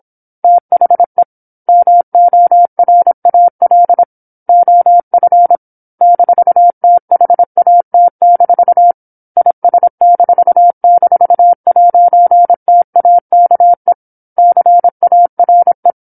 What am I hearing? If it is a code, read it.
Morse code, heard as a keyed tone, about 26 wpm, 700 Hz. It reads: THE MORAL OF -THAT- IS--'TAKE CARE
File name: 08-Chapter-08 26 WPM